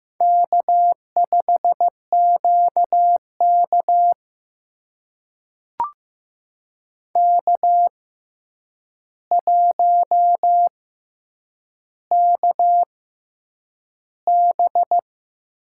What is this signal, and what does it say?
Morse code, heard as a keyed tone, about 15 wpm, spaced as for 8 wpm, 700 Hz. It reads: K5QK K1KB
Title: USA Call Signs - 4 Characters - 6x Character Spacing - 15wpm